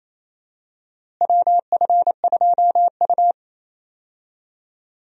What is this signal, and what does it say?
Morse code, heard as a keyed tone, about 28 wpm, 700 Hz. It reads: WF2U
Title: USA Call Signs - 4 Characters - Repeated 3x - 28wpm